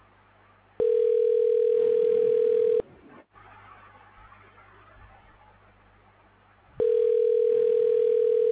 Ringback tone (North America), 440 + 480 Hz.